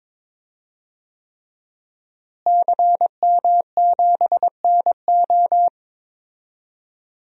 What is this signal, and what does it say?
Morse code, heard as a keyed tone, about 21 wpm, 700 Hz. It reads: CM7NO
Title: DXCC-100 Call Signs - 5 Characters - Repeated 3x - 22wpm